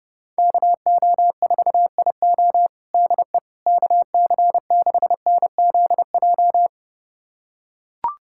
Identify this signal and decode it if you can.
Morse code, heard as a keyed tone, about 30 wpm, 700 Hz. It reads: KO4IO DE KC6NZJ
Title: USA Callsign DE Callsign 30wpm